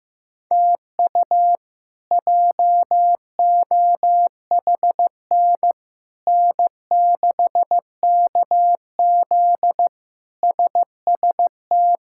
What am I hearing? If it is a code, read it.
Morse code, heard as a keyed tone, about 15 wpm, 700 Hz. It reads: TU JOHN N6KZ SST
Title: K1USN SST QSO Practice - 15wpm